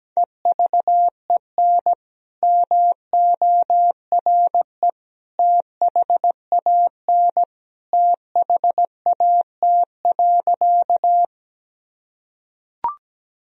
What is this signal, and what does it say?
Morse code, heard as a keyed tone, about 17 wpm, 700 Hz. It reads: EVEN MORE THAN THAT.
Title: Sentences from Top 1000 Words - Review Part 1 - 17wpm